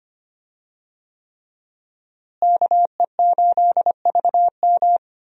Morse code, heard as a keyed tone, about 25 wpm, 700 Hz. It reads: KE8VM
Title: USA Call Signs - 5 Characters - Repeated 3x - 25wpm